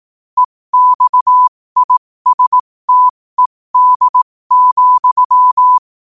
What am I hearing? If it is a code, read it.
Morse code, heard as a keyed tone, about 18 wpm, spaced as for 15 wpm, 1000 Hz. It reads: EXISTED,
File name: qotdfiles / 15wpm